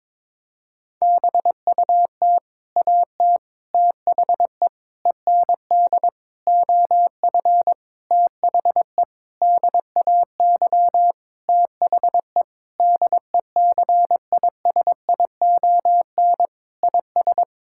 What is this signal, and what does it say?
Morse code, heard as a keyed tone, about 22 wpm, 700 Hz. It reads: BUT AT THE END OF THE DAY THE DECISION IS
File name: practice / 22wpm / Sentences with 8-Word Phrases - 22wpm